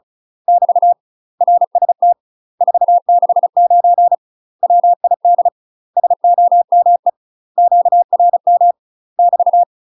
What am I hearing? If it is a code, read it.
Morse code, heard as a keyed tone, about 36 wpm, 700 Hz. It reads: <BT> RST 469 WID SOME QRM <BT>
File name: UK MASTER QSOs TRAINING / 35WPM / UK MASTER QSOs 0023